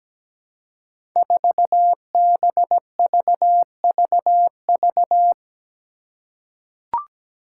Morse code, heard as a keyed tone, about 17 wpm, 700 Hz. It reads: 4BVVV
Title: Five Challenging Letter-Number Combinations 17wpm